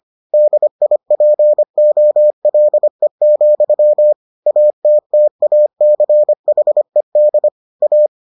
Morse code, heard as a keyed tone, about 25 wpm, 600 Hz. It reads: DIPOLE, ATTACHED A